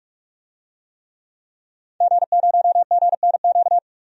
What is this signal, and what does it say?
Morse code, heard as a keyed tone, about 43 wpm, 700 Hz. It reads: G0GNQ
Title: DXCC-100 Call Signs - 5 Characters - 45wpm